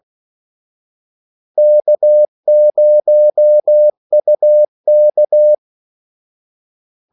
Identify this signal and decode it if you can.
Morse code, heard as a keyed tone, about 16 wpm, 600 Hz. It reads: K0UK
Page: Kaldesignaler 16 wpm